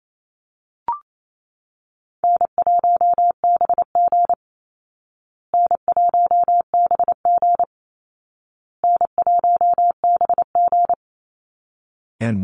Morse code, heard as a keyed tone, about 28 wpm, 700 Hz. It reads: N1BG N1BG N1BG
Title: Call Signs Repeated 2x - 28wpm